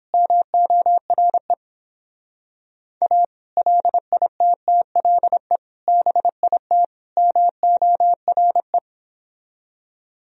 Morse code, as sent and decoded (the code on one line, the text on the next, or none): -- --- .-. / .- / .-.. .. - - .-.. / -... .. - / -- --- .-. .
MORE A LITTLE BIT MORE